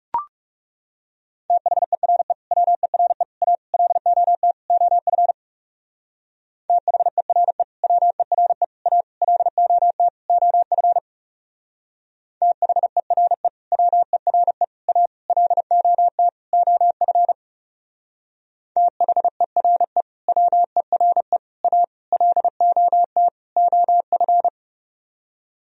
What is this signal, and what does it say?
Morse code, heard as a keyed tone, about 37 wpm, 700 Hz. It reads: THERE WERE A LOT OF THERE WERE A LOT OF THERE WERE A LOT OF THERE WERE A LOT OF
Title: Speed-Racing - Sets of 5 Words 30wpm